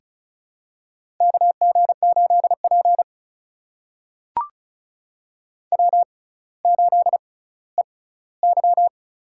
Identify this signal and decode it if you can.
Morse code, heard as a keyed tone, about 35 wpm, spaced as for 18 wpm, 700 Hz. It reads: KG8P W8EY
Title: USA Call Signs - 4 Characters - 6x Character Spacing - 35wpm